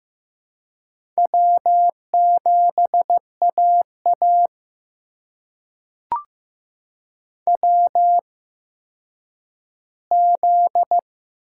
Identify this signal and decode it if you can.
Morse code, heard as a keyed tone, about 15 wpm, spaced as for 9 wpm, 700 Hz. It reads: W7AA WZ
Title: USA Call Signs - 4 Characters - 8x Character Spacing - 15wpm